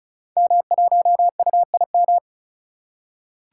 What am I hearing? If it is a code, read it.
Morse code, heard as a keyed tone, about 36 wpm, 700 Hz. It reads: M1UIM
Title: Code Groups - 5 Letter:Numbers - 35wpm